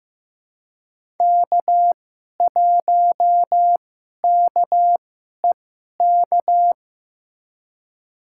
Morse code, heard as a keyed tone, about 15 wpm, spaced as for 11 wpm, 700 Hz. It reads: K1KEK
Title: USA Call Signs - 5 Characters - 2x Character Spacing - 15wpm